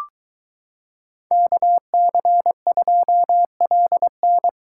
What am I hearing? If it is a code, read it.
Morse code, heard as a keyed tone, about 23 wpm, 700 Hz. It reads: KC2LN